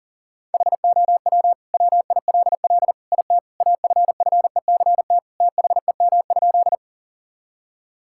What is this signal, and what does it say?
Morse code, heard as a keyed tone, about 40 wpm, 700 Hz. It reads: HOW WILL IT AFFECT THEM?
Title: Sentences from Top 1000 Words - Review Part 6 - 40wpm